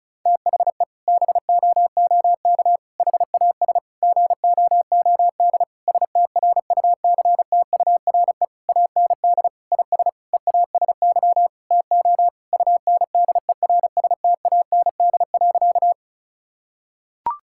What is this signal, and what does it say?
Morse code, heard as a keyed tone, about 36 wpm, 700 Hz. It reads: THE BOOK HAS GOOD STRUCTURE AND IS EASY TO UNDERSTAND.
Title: Sentences from Top 1000 Words - Review Part 6 - 35wpm